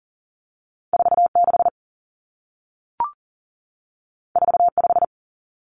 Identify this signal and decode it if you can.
Morse code, heard as a keyed tone, about 40 wpm, 700 Hz. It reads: 46 45
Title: Common Cut Numbers - 2-Digit 40wpm